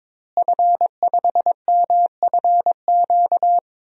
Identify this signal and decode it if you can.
Morse code, heard as a keyed tone, about 22 wpm, 700 Hz. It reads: F5MFQ
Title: DXCC-100 Call Signs - 5 Characters - Repeated 2x - 22wpm